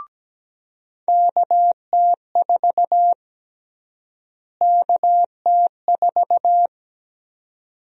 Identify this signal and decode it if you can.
Morse code, heard as a keyed tone, about 17 wpm, 700 Hz. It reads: KT4 KT4
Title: USA Call Sign Prefixes - Repeated 2x - 17wpm